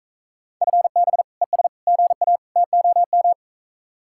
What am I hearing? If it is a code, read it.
Morse code, heard as a keyed tone, about 44 wpm, 700 Hz. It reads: FB ES GA TOM